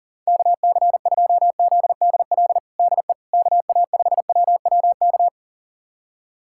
Morse code, heard as a keyed tone, about 40 wpm, 700 Hz. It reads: KC2ZDL DE KA5WWK
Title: USA Callsign DE Callsign 40wpm